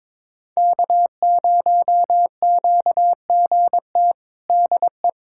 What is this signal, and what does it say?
Morse code, heard as a keyed tone, about 22 wpm, 700 Hz. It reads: K0QGT DE
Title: USA Callsign DE Callsign - 5 Characters - 22wpm